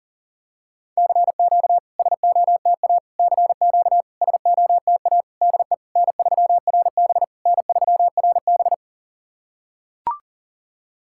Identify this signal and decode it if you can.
Morse code, heard as a keyed tone, about 40 wpm, 700 Hz. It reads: CQ SOTA CQ SOTA DE N3RB N3RB
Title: SOTA QSO Practice 40wpm